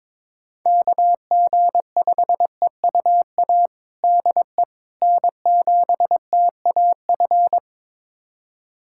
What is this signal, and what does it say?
Morse code, heard as a keyed tone, about 22 wpm, 700 Hz. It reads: KG5EUA DE N7TAF
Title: USA Callsign DE Callsign 22wpm